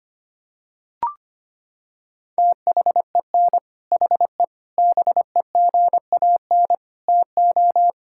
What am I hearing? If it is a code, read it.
Morse code, heard as a keyed tone, about 25 wpm, 700 Hz. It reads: THEN HE BEGAN TO
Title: Sentences from Top 1000 Words - Review Part 5 - 25wpm